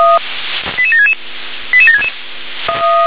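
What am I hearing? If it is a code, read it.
Five-tone selective call (ZVEI), heard as quick runs of five tones, tones of about 70 ms. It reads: 88766 88766
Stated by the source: ZVEI five-tone selective call received on SDR.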